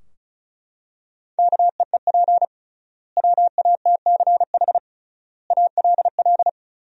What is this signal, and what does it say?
Morse code, heard as a keyed tone, about 36 wpm, 700 Hz. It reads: KEEP WATCH ALL
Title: 3x Word-Spacing - Sentences from Top 300 Words - 35wpm